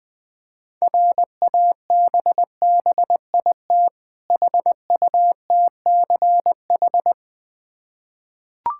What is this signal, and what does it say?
Morse code, heard as a keyed tone, about 20 wpm, 700 Hz. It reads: RABBIT HUTCH
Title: Difficult Words with Hints - 20wpm